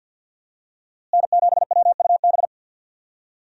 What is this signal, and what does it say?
Morse code, heard as a keyed tone, about 52 wpm, 700 Hz. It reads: N7WUB
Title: USA Call Signs - 5 Characters - Repeated 2x - 50wpm